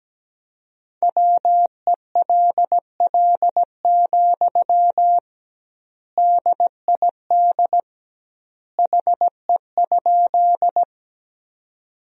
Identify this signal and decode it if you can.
Morse code, heard as a keyed tone, about 17 wpm, 700 Hz. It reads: WELL, DID HE?
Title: 2x Word-Spacing - Sentences from Top 200 Words - 17wpm